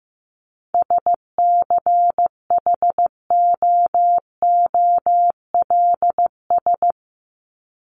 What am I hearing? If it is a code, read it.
Morse code, heard as a keyed tone, about 15 wpm, 700 Hz. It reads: SCHOOLS
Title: ES:S Suffix - Top 100 Words with Base Words 15wpm